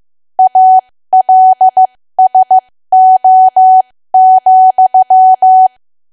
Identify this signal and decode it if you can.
Morse code, heard as a keyed tone, about 15 wpm, spaced as for 13 wpm, 750 Hz. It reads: ALSO,
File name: cwa / W1AW-Level2 / 13WPM